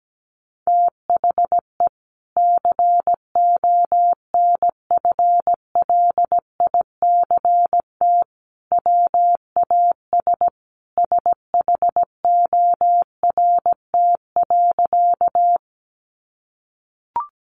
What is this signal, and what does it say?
Morse code, heard as a keyed tone, about 17 wpm, 700 Hz. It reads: THE CONFLICT WAS SHORT.
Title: Sentences from Top 1300 Words 17wpm